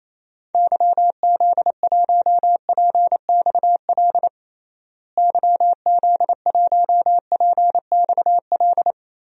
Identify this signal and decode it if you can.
Morse code, heard as a keyed tone, about 28 wpm, 700 Hz. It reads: YZ1PXL YZ1PXL
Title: DXCC-100 Call Signs - 6 Characters - Repeated 3x - 28wpm